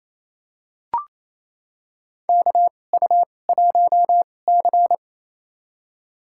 Morse code, heard as a keyed tone, about 28 wpm, spaced as for 20 wpm, 700 Hz. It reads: KU1C
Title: USA Call Signs - 4 Characters - 2x Character Spacing - 28wpm